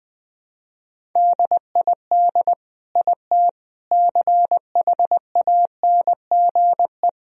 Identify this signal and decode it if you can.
Morse code, heard as a keyed tone, about 20 wpm, 700 Hz. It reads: DID IT CHANGE
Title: Sentences from Top 1000 Words - Review Part 1 - 20wpm